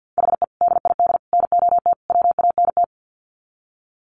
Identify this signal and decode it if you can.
Morse code, heard as a keyed tone, about 50 wpm, 700 Hz. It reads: HE DID NOT WANT
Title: Sets of 4 Words 50wpm